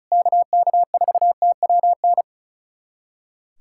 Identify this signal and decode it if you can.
Morse code, heard as a keyed tone, about 34 wpm, 700 Hz. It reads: KK4TWN
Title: USA Callsign DE Callsign - 6 Characters - 35wpm